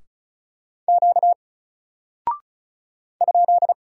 Morse code, heard as a keyed tone, about 34 wpm, 700 Hz. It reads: Q ?
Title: Single Character (Letter:Number:Punctuation:Prosign) Alt 35wpm